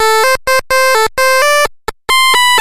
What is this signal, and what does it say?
There is no code.